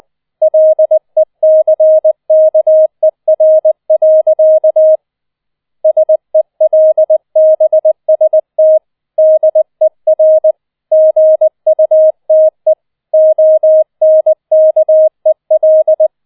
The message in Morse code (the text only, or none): LECKER. SELBST DER GUTE ONKEL